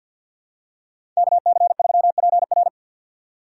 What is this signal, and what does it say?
Morse code, heard as a keyed tone, about 52 wpm, 700 Hz. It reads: KC3PR
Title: USA Call Signs - 3x Character Spacing - 50wpm